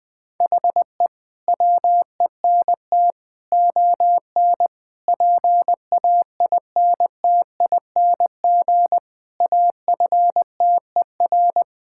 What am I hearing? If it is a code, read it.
Morse code, heard as a keyed tone, about 21 wpm, 700 Hz. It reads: HE WENT ON PAINTING AFTER
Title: Sentences from Top 1000 Words - Review Part 6 - 20wpm